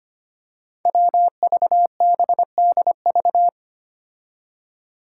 Morse code, heard as a keyed tone, about 25 wpm, 700 Hz. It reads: WVBDV
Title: Five Challenging Letter-Number Combinations 25wpm